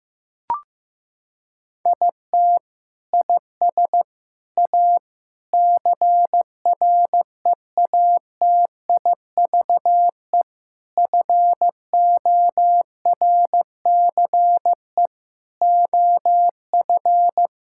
Morse code, heard as a keyed tone, about 15 wpm, 700 Hz. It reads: IT IS A CREATIVE FORCE OF
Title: Sentences from Top 1000 Words - Review Part 5 - 15wpm